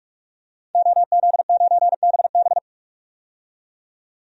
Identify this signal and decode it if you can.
Morse code, heard as a keyed tone, about 46 wpm, 700 Hz. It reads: OZ9BB
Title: DXCC-100 Call Signs - 5 Characters - 45wpm